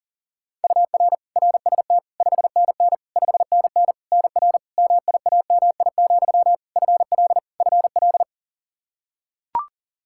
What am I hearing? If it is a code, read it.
Morse code, heard as a keyed tone, about 40 wpm, 700 Hz. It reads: UR RST 5NN 5NN NR MIAMI, FL FL
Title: K1USN SST QSO Practice 40wpm